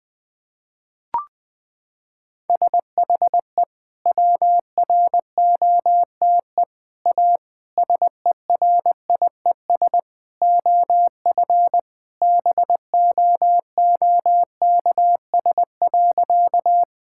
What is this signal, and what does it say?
Morse code, heard as a keyed tone, about 21 wpm, 700 Hz. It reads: SHE WROTE A SERIES OF BOOKS.
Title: Sentences with 3-Word Phrases - 20wpm